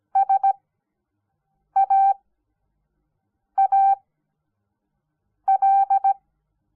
Morse code, heard as a keyed tone, about 17 wpm, spaced as for 5 wpm, 780 Hz. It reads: SAAL
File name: Onlinelehrgang / morsen / DJ4UF Morsen06